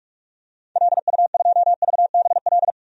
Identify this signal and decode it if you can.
Morse code, heard as a keyed tone, about 45 wpm, 700 Hz. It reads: LU2VBL
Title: DXCC-100 Call Signs - 6 Characters - 45wpm